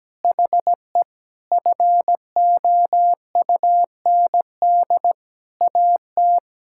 Morse code, heard as a keyed tone, about 17 wpm, 700 Hz. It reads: HE FOUND AT
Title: Sentences from Top 1000 Words - Review Part 6 - 17wpm